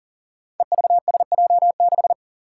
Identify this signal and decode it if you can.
Morse code, heard as a keyed tone, about 40 wpm, 700 Hz. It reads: EVSJ6
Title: Five Challenging Letter-Number Combinations 40wpm